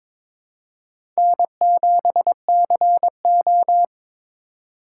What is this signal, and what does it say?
Morse code, heard as a keyed tone, about 22 wpm, 700 Hz. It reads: N7CO
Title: USA Call Signs - 4 Characters - 3x Character Spacing - 22wpm